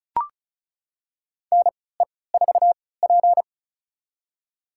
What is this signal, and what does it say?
Morse code, heard as a keyed tone, about 36 wpm, spaced as for 20 wpm, 700 Hz. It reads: NE4P
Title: USA Call Signs - 4 Characters - 3x Character Spacing - 35wpm